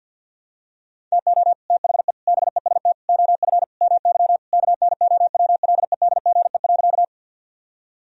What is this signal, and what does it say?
Morse code, heard as a keyed tone, about 52 wpm, 700 Hz. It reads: TO THE BEST OF MY KNOWLEDGE.
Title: Sentences from Top 1000 Words - Review Part 3 - 50wpm